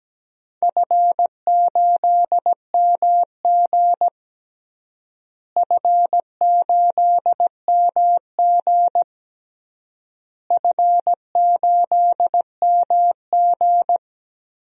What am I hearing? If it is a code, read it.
Morse code, heard as a keyed tone, about 17 wpm, 700 Hz. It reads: F8MG F8MG F8MG
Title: DXCC-100 Call Signs - 4 Characters - Repeated 3x - 17wpm